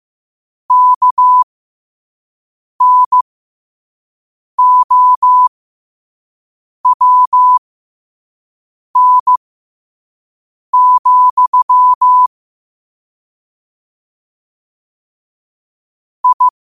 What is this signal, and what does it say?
Morse code, heard as a keyed tone, about 15 wpm, spaced as for 5 wpm, 1000 Hz. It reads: KNOWN, I
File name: 5wpm